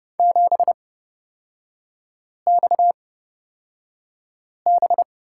Morse code, heard as a keyed tone, about 31 wpm, spaced as for 5 wpm, 690 Hz. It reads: 7XB
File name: Lesson 37e - Letter x - Three Characters 30wpm